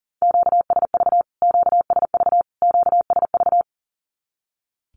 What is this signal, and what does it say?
Morse code, heard as a keyed tone, about 40 wpm, 700 Hz. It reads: QSV QSV QSV
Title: Q-Codes Repeated 3x 40wpm